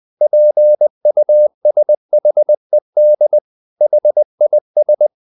Morse code, heard as a keyed tone, about 20 wpm, 600 Hz. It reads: PUSHED HIS
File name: the old man and the sea part 2 20wpm